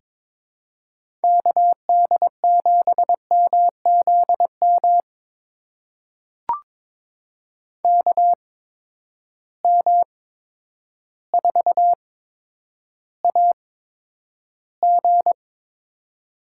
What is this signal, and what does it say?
Morse code, heard as a keyed tone, about 22 wpm, spaced as for 10 wpm, 700 Hz. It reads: KD7MZM KM4AG
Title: USA Call Signs - 8x Character Spacing - 22wpm